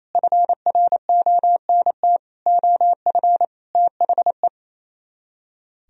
Morse code, heard as a keyed tone, about 33 wpm, 700 Hz. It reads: FRONT OF THE
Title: Speed-Racing - Sets of 5 Words 28wpm